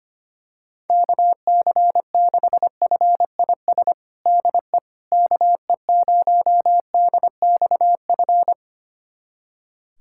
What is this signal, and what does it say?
Morse code, heard as a keyed tone, about 25 wpm, 700 Hz. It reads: KC6FIS DE KE0DXF
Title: USA Callsign DE Callsign - 6 Characters - 25wpm